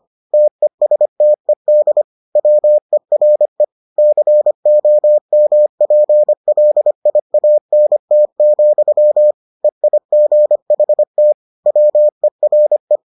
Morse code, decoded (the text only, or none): TESTED WERE COMPLIANT, EIGHT WERE